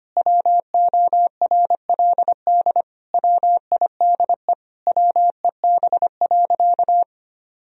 Morse code, heard as a keyed tone, about 25 wpm, 700 Hz. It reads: WORLD WIDE WEB.
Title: Sentences from Top 1000 Words - Review Part 4 - 25wpm